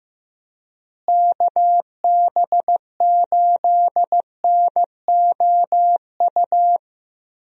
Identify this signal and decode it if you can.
Morse code, heard as a keyed tone, about 15 wpm, 700 Hz. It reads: KB8NOU
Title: USA Call Signs - 6 Characters - Repeated 2x - 15wpm